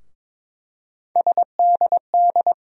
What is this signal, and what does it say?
Morse code, heard as a keyed tone, about 22 wpm, 700 Hz. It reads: SDD
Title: USA Call Sign Postfixes - 3 Letters 22wpm